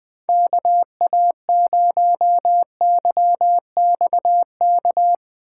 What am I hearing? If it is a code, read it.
Morse code, heard as a keyed tone, about 20 wpm, 700 Hz. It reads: KA0YXK
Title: USA Call Signs - 3x Character Spacing - 20wpm